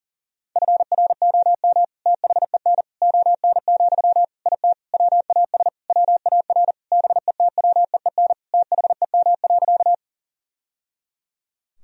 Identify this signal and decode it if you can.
Morse code, heard as a keyed tone, about 40 wpm, 700 Hz. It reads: FROM THEN ON, IT WAS WAR BETWEEN THEM.
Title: Sentences from Top 1000 Words - Review Part 2 - 40wpm